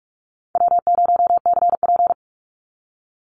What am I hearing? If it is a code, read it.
Morse code, heard as a keyed tone, about 44 wpm, 690 Hz. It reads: W0CP
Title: NA SOTA Chaser Callsigns 45wpm